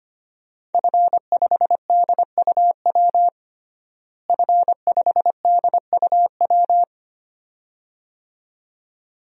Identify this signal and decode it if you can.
Morse code, heard as a keyed tone, about 25 wpm, 700 Hz. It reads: F5DUW F5DUW
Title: DXCC-100 Call Signs - 5 Characters - Repeated 2x - 25wpm